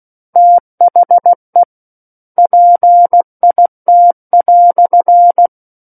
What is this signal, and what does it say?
Morse code, heard as a keyed tone, about 16 wpm, 700 Hz. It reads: THE PIT"
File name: Book One / 07-Chapter 7 - 16 WPM